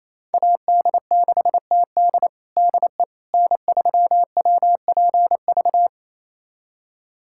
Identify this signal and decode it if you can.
Morse code, heard as a keyed tone, about 28 wpm, 700 Hz. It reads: AD6TD DE N3WPV
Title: USA Callsign DE Callsign - 5 Characters - 28wpm